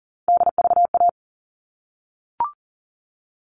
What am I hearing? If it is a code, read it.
Morse code, heard as a keyed tone, about 40 wpm, 700 Hz. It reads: DVA
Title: All Cut Numbers - 3-Digit 40wpm